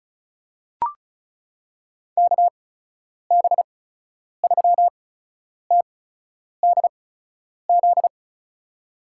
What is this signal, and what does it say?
Morse code, heard as a keyed tone, about 36 wpm, spaced as for 10 wpm, 700 Hz. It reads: KB3TDZ
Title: USA Call Signs - 6 Characters - 8x Character Spacing - 35wpm